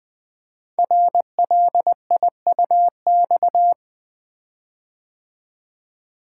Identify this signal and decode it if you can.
Morse code, heard as a keyed tone, about 20 wpm, 700 Hz. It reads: RLIUX
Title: Code Groups - 5 Letter:Numbers - 20wpm